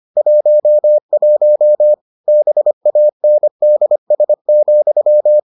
Morse code, heard as a keyed tone, about 25 wpm, 600 Hz. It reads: 11 BANDS,